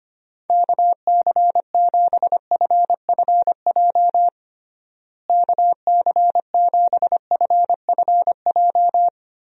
Morse code, heard as a keyed tone, about 25 wpm, 700 Hz. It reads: KC7FFJ KC7FFJ
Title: USA Call Signs - 6 Characters - Repeated 2x - 25wpm